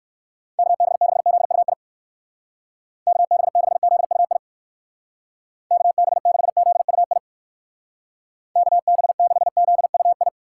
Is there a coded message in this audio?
Yes: Morse code from 0.6 to 10.3 s, heard as a keyed tone, about 58 wpm, 700 Hz. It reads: KB6ZUI KB6ZUI KB6ZUI KB6ZUI